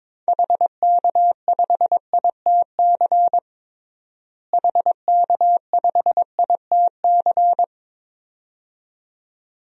Morse code, heard as a keyed tone, about 22 wpm, 700 Hz. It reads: HK5ITC HK5ITC
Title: DXCC-100 Call Signs - 6 Characters - Repeated 2x - 22wpm